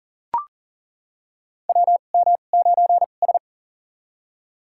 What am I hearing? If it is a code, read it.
Morse code, heard as a keyed tone, about 42 wpm, spaced as for 29 wpm, 690 Hz. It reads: WM9S
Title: USA Call Signs - 4 Characters - 2x Character Spacing - 40wpm